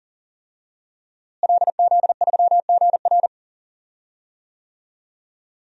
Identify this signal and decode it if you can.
Morse code, heard as a keyed tone, about 40 wpm, 700 Hz. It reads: LZ3GR
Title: DXCC-100 Call Signs - 5 Characters - Repeated 3x - 40wpm